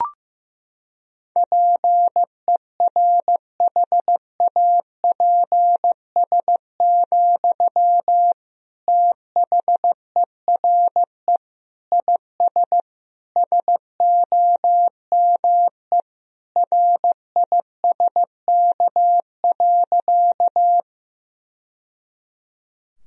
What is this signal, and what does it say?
Morse code, heard as a keyed tone, about 15 wpm, 700 Hz. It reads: PERHAPS, THERE IS SOME RISK.
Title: Sentences from Top 1000 Words - Review Part 4 - 15wpm